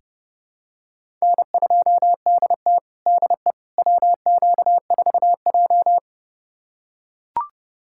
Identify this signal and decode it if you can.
Morse code, heard as a keyed tone, about 32 wpm, 700 Hz. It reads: N2DT DE WQ4J
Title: USA Callsign DE Callsign - 4 Characters - 30wpm